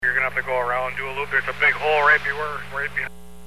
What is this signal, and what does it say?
Emergency Alert System (SAME) tones, heard as harsh buzzing bursts, with speech. It is heard from 0.0 to 3.2 s.